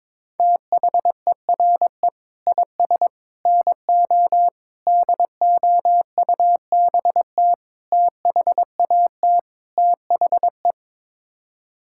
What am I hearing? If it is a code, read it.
Morse code, heard as a keyed tone, about 22 wpm, 700 Hz. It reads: THERE IS NO DOUBT THAT THE
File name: practice / 22wpm / Sets of 6 Words - 22wpm